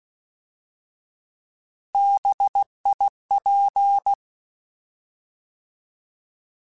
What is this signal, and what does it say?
Morse code, heard as a keyed tone, about 16 wpm, 800 Hz. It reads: BIP
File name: Alphabetical / Alphabetical mp3s / Bs @15